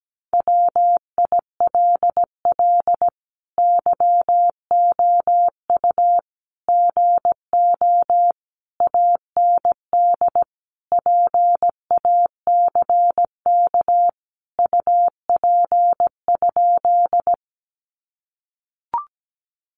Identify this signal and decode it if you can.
Morse code, heard as a keyed tone, about 17 wpm, 700 Hz. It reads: WILL YOU GO AND PACK UP?
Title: Sentences from Top 1300 Words 17wpm